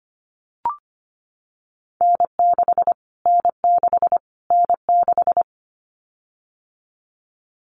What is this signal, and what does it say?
Morse code, heard as a keyed tone, about 25 wpm, 700 Hz. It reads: N6 N6 N6
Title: Callsign Prefixes for Top 25 DXCC - Repeated 3x - 25wpm